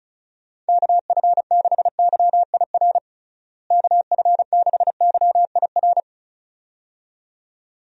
Morse code, heard as a keyed tone, about 36 wpm, 700 Hz. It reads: KF6YIR KF6YIR
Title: USA Call Signs - 6 Characters - Repeated 2x - 35wpm